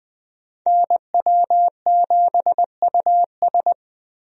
Morse code, heard as a keyed tone, about 20 wpm, 700 Hz. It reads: NW7US
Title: USA Call Signs - 5 Characters - Repeated 3x - 20wpm